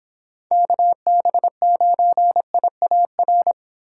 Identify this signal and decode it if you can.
Morse code, heard as a keyed tone, about 27 wpm, 700 Hz. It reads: KB9IAR
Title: Speed Racing - USA Call Signs 22wpm